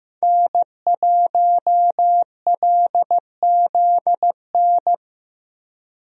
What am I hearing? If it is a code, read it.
Morse code, heard as a keyed tone, about 15 wpm, 700 Hz. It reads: N1LZN
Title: USA Call Signs - 5 Characters - Repeated 3x - 15wpm